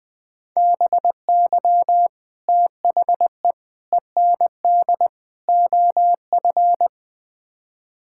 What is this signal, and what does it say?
Morse code, heard as a keyed tone, about 20 wpm, 700 Hz. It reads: BY THE END OF
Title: Build A Phrase - 4-Word Phrases - 20wpm